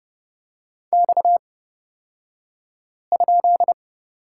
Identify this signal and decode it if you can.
Morse code, heard as a keyed tone, about 31 wpm, spaced as for 5 wpm, 700 Hz. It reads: X?
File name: Lesson 37d - Letter x - Two Characters 30wpm